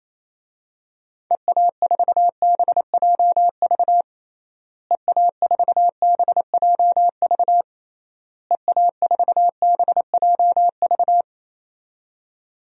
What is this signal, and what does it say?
Morse code, heard as a keyed tone, about 28 wpm, 700 Hz. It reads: EA4BJV EA4BJV EA4BJV
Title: DXCC-100 Call Signs - 6 Characters - Repeated 3x - 28wpm